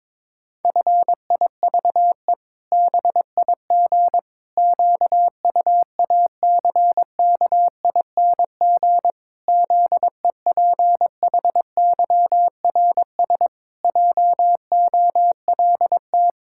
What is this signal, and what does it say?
Morse code, heard as a keyed tone, about 21 wpm, 700 Hz. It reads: FIVE BIG QUACKING ZEPHYRS JOLT
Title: Pangram Challenge v2 - 22wpm